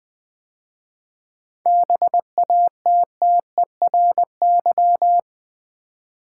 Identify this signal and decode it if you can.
Morse code, heard as a keyed tone, about 20 wpm, 700 Hz. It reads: BATTERY
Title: 7 Letter Words - First 3 Letters Given - 20wpm